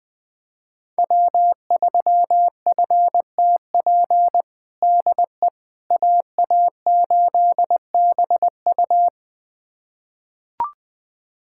Morse code, heard as a keyed tone, about 20 wpm, 700 Hz. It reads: W3FTP DE AA8BU
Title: USA Callsign DE Callsign - 5 Characters - 20wpm